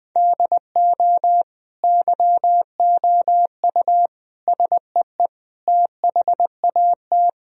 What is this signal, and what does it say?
Morse code, heard as a keyed tone, about 20 wpm, 700 Hz. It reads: DO YOU SEE THAT
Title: Sentences from Top 1000 Words - Review Part 3 - 20wpm